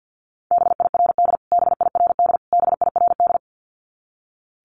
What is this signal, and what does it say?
Morse code, heard as a keyed tone, about 50 wpm, 700 Hz. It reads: BIRD BIRD BIRD
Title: Top 500 Words Repeated 50wpm